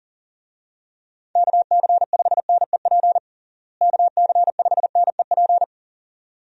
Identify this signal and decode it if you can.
Morse code, heard as a keyed tone, about 40 wpm, 700 Hz. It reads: KC5NEP KC5NEP
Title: USA Call Signs - 6 Characters - Repeated 2x - 40wpm